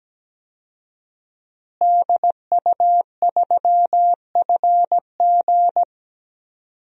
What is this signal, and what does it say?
Morse code, heard as a keyed tone, about 17 wpm, 700 Hz. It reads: DU3FG
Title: DXCC-100 Call Signs - 5 Characters - Repeated 2x - 17wpm